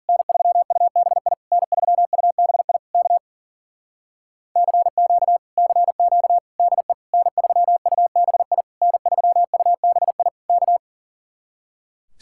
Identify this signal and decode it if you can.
Morse code, heard as a keyed tone, about 52 wpm, 700 Hz. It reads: N3UBI N3UBI K CQ CQ DE N3UBI N3UBI K